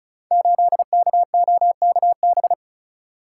Morse code, heard as a keyed tone, about 36 wpm, 700 Hz. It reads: 8KOKB
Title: Code Groups - 5 Letter:Numbers - 35wpm